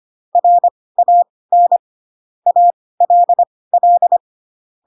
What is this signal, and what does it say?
Morse code, heard as a keyed tone, about 25 wpm, spaced as for 18 wpm, 700 Hz. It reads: RAN ALL
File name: Pinocchio ch3 12 25 ik1hge